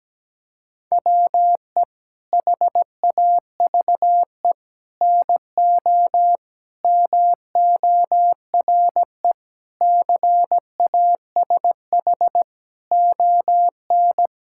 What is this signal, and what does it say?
Morse code, heard as a keyed tone, about 17 wpm, 700 Hz. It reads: WE HAVE NO MORE CASH ON
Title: Sentences from Top 1000 Words - Review Part 5 - 17wpm